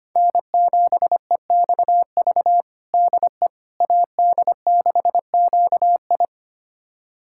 Morse code, heard as a keyed tone, about 25 wpm, 700 Hz. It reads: N7EXV DE AD6QI
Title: USA Callsign DE Callsign - 5 Characters - 25wpm